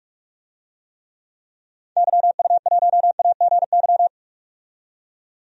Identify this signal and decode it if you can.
Morse code, heard as a keyed tone, about 43 wpm, 700 Hz. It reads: YU1AGY
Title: DXCC-100 Call Signs - 6 Characters - 45wpm